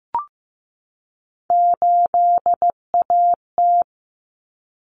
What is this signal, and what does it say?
Morse code, heard as a keyed tone, about 15 wpm, 700 Hz. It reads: 8AT
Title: Common Cut Numbers - 3-Digit 15wpm